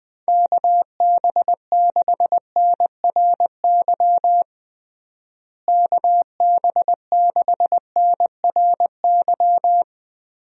Morse code, heard as a keyed tone, about 20 wpm, 700 Hz. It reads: KB6NRY KB6NRY
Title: USA Call Signs - 6 Characters - Repeated 3x - 20wpm